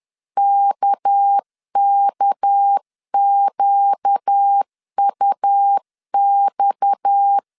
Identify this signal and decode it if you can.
Morse code, heard as a keyed tone, about 10 wpm, 800 Hz. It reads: KKQUX
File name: CW / DET / 1 6 1500C S C 10wpm 4252